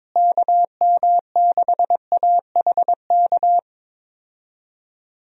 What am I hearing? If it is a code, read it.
Morse code, heard as a keyed tone, about 22 wpm, 700 Hz. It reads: KM6AHK
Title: USA Call Signs - 6 Characters - Repeated 3x - 22wpm